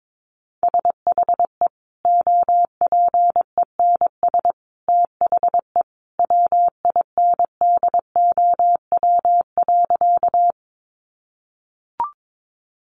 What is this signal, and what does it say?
Morse code, heard as a keyed tone, about 22 wpm, 700 Hz. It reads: SHE OPENS THE WINDOW.
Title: Easy Phrases 22wpm